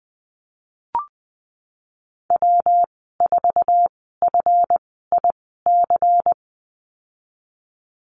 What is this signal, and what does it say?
Morse code, heard as a keyed tone, about 20 wpm, spaced as for 15 wpm, 700 Hz. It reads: W4FIC
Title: 173 Lesson 28g - Number 9 - Call Signs 20wpm